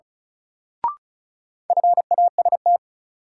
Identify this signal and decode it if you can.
Morse code, heard as a keyed tone, about 34 wpm, 700 Hz. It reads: FAST
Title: Sentences from Top 400 Words - New Word Drill - 35wpm